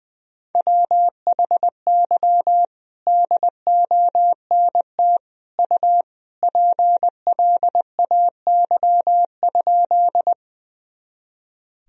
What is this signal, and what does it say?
Morse code, heard as a keyed tone, about 20 wpm, 700 Hz. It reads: WHY DONT U PLAY?
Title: Sentences from Top 300 Words and Abbreviations 20wpm